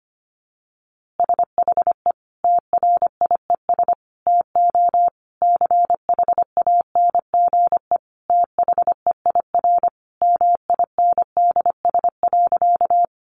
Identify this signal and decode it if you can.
Morse code, heard as a keyed tone, about 25 wpm, 700 Hz. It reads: SHE TRIES TO CHANGE THEIR MINDS.
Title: Easy Phrases 25wpm